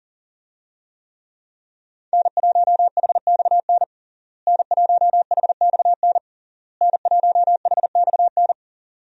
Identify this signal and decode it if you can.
Morse code, heard as a keyed tone, about 40 wpm, 700 Hz. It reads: N1HXN N1HXN N1HXN
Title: USA Call Signs - 5 Characters - Repeated 3x - 40wpm